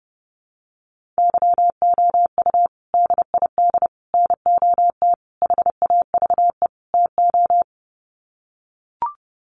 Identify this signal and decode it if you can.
Morse code, heard as a keyed tone, about 28 wpm, 700 Hz. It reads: YOU DID NOT HAVE TO
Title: Sets of 5 Words Encore 30wpm